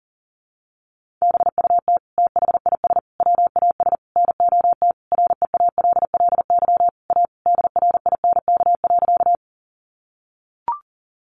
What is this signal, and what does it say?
Morse code, heard as a keyed tone, about 40 wpm, 700 Hz. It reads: BUT THIS WAS NOT REALLY A DRINK.
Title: Sentences from Top 1500 Words 40wpm